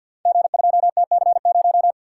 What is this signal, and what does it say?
Morse code, heard as a keyed tone, about 53 wpm, 700 Hz. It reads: G2TC0
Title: Code Groups - 5 Letter:Numbers - 50wpm